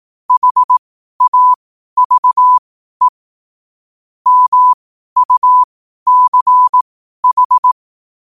Morse code, heard as a keyed tone, about 18 wpm, spaced as for 12 wpm, 1000 Hz. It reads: HAVE MUCH
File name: qotdfiles / 12wpm